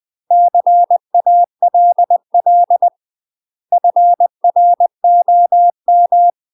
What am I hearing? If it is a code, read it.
Morse code, heard as a keyed tone, about 20 wpm, 700 Hz. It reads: CALL FROM
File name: UK MASTER QSOs TRAINING / 20WPM / UK MASTER QSOs 0066